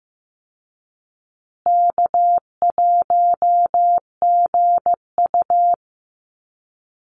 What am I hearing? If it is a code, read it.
Morse code, heard as a keyed tone, about 15 wpm, 700 Hz. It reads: K1GU
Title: Easy Call Signs Repeated 3x - 15wpm